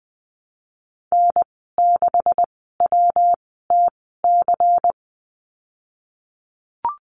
Morse code, heard as a keyed tone, about 20 wpm, spaced as for 15 wpm, 700 Hz. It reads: N6WTC
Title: 173 Lesson 28g - Number 9 - Call Signs 20wpm